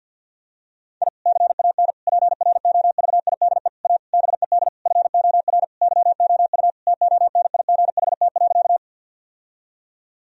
Morse code, heard as a keyed tone, about 50 wpm, 700 Hz. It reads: I CAN PROVIDE A BED FOR YOU TONIGHT.
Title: Sentences from Top 1000 Words - Review Part 3 - 50wpm